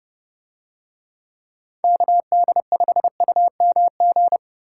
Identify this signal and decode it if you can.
Morse code, heard as a keyed tone, about 27 wpm, 700 Hz. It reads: KD5UMG